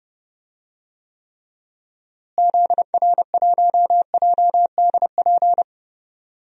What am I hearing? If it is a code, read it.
Morse code, heard as a keyed tone, about 30 wpm, 700 Hz. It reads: ZR1JDP
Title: DXCC-100 Call Signs - 6 Characters - Repeated 2x - 30wpm